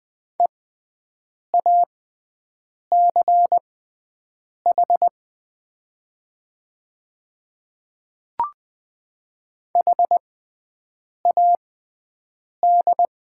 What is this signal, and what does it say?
Morse code, heard as a keyed tone, about 20 wpm, spaced as for 7 wpm, 700 Hz. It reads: EACH HAD
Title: 6x Character Spacing - Top 100 Words - 20wpm